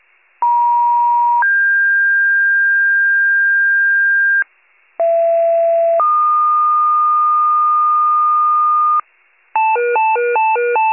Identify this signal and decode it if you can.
Two-tone page, heard as a one-second tone, then a three-second tone, about 950 then 1600 Hz, about 670 then 1220 Hz, with noise. It is heard from 0.4 to 9.0 s.